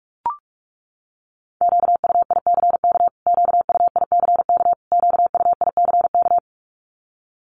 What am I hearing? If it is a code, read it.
Morse code, heard as a keyed tone, about 44 wpm, 700 Hz. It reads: QUICK QUICK QUICK
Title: Top 500 Words Repeated 45wpm